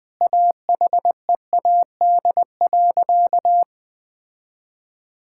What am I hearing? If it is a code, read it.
Morse code, heard as a keyed tone, about 20 wpm, 700 Hz. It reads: AHEAD.
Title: Sentences with 7-Word Phrases - Spoken Beginning - 20wpm